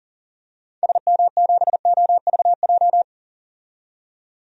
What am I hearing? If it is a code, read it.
Morse code, heard as a keyed tone, about 40 wpm, 700 Hz. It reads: SM7OVJ
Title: DXCC-100 Call Signs - 6 Characters - 40wpm